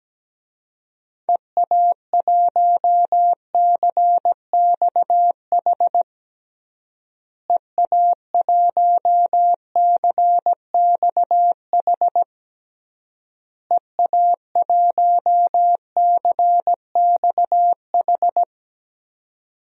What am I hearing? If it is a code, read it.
Morse code, heard as a keyed tone, about 17 wpm, 700 Hz. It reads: EA1CXH EA1CXH EA1CXH
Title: DXCC-100 Call Signs - 6 Characters - Repeated 3x - 17wpm